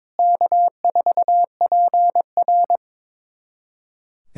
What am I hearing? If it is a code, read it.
Morse code, heard as a keyed tone, about 22 wpm, 700 Hz. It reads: K4PR
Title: USA Callsign DE Callsign - 4 Characters - 22wpm